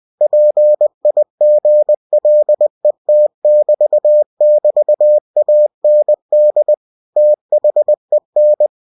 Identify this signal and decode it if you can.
Morse code, heard as a keyed tone, about 20 wpm, 600 Hz. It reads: PIGLET<BT><BT>AND THEN
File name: cw-books / winnie-the-pooh / winnie-20wpm-page-0072